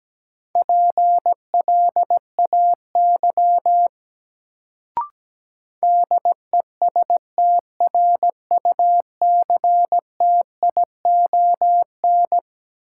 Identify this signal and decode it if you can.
Morse code, heard as a keyed tone, about 17 wpm, 700 Hz. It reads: PLAY DESTRUCTION
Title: Sets of 5 Words Encore - New Word Drill - 17wpm